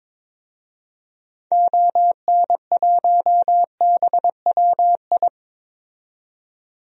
Morse code, heard as a keyed tone, about 22 wpm, 700 Hz. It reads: ON1BWI
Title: DXCC-100 Call Signs - 6 Characters - 22wpm